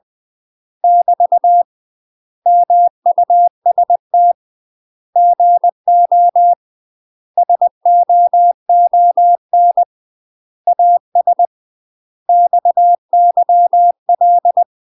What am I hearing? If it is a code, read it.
Morse code, heard as a keyed tone, about 20 wpm, 700 Hz. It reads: <BT> MUST GO SOON AS XYL
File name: UK MASTER QSOs TRAINING / 20WPM / UK MASTER QSOs 0041